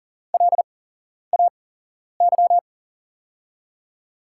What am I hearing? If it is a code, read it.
Morse code, heard as a keyed tone, about 40 wpm, spaced as for 11 wpm, 700 Hz. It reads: LAY
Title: 8x Character Spacing - Top 500 Words - 40wpm